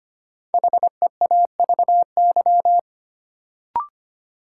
Morse code, heard as a keyed tone, about 25 wpm, 700 Hz. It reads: HEAVY
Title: Sentences from Top 1000 Words - Review Part 6 - New Word Drill - 25wpm